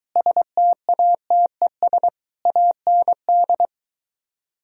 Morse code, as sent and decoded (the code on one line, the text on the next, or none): ... - .- - . ... / .- -. -..
STATES AND